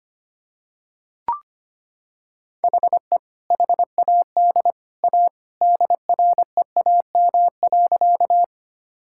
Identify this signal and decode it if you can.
Morse code, heard as a keyed tone, about 25 wpm, 700 Hz. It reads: HE HAD A DREAM.
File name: practice / 25wpm / Sentences with 2-Word Phrases - 25wpm